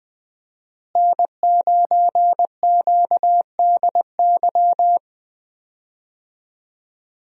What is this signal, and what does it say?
Morse code, heard as a keyed tone, about 20 wpm, 700 Hz. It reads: N9QDY
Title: USA Call Signs - 5 Characters - Repeated 3x - 20wpm